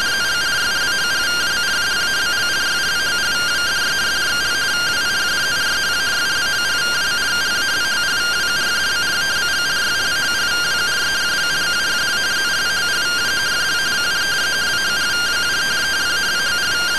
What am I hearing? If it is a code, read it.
Two-tone teleprinter signal, about 170 Hz shift.
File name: findings / MR AXIAL fr1 120.gen-10 rtty rx